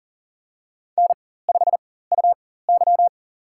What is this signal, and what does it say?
Morse code, heard as a keyed tone, about 38 wpm, spaced as for 18 wpm, 690 Hz. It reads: N5UY
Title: USA Call Signs - 4x Character Spacing - 40wpm